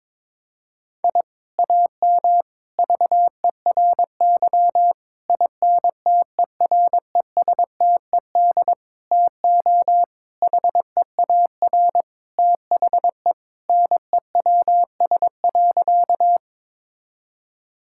Morse code, heard as a keyed tone, about 22 wpm, 700 Hz. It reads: I AM VERY INTERESTED TO HEAR THE NEWS.
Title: Sentences from Top 1000 Words - Review Part 6 - 22wpm